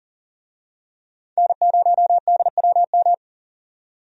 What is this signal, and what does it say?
Morse code, heard as a keyed tone, about 37 wpm, 700 Hz. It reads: N0DWM
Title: USA Call Signs - 5 Characters - 40wpm